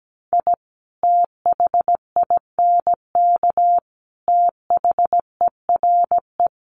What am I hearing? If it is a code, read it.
Morse code, heard as a keyed tone, about 17 wpm, 700 Hz. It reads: I THINK THERE
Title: Sets of 3 Words 17wpm